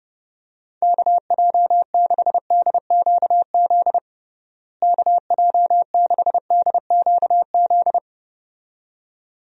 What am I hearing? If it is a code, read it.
Morse code, heard as a keyed tone, about 30 wpm, 700 Hz. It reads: KJ6DQZ KJ6DQZ
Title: USA Call Signs - 6 Characters - Repeated 2x - 30wpm